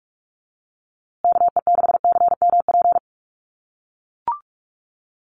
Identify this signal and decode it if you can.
Morse code, heard as a keyed tone, about 44 wpm, 700 Hz. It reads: KE6CMP
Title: USA Call Signs Repeated 3x 45wpm